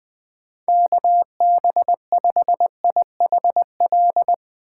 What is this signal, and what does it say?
Morse code, heard as a keyed tone, about 22 wpm, 700 Hz. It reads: KB5IHL
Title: Speed Racing - USA 2x3 Call Signs 20wpm